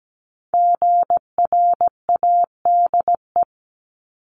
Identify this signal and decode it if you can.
Morse code, heard as a keyed tone, about 17 wpm, 700 Hz. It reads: GRADE
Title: Multi-letter Phonemes 17wpm